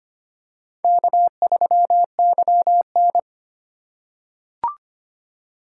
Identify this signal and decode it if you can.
Morse code, heard as a keyed tone, about 25 wpm, 700 Hz. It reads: K3YN
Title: USA Call Signs - 4x Character Spacing - 25wpm